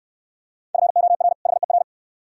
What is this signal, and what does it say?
Morse code, heard as a keyed tone, about 69 wpm, 700 Hz. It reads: FOR HER